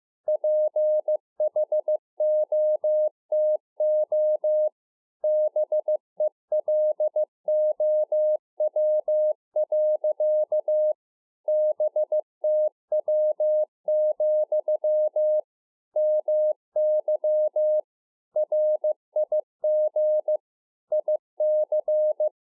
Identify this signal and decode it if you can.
Morse code, heard as a keyed tone, about 15 wpm, 610 Hz. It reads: PHOTO BELOW. BTW, MY RIG IC
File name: 15WPM